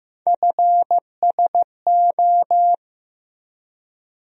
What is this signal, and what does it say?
Morse code, heard as a keyed tone, about 15 wpm, 700 Hz. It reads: FSO
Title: USA Call Sign Postfixes - 3 Letters 15wpm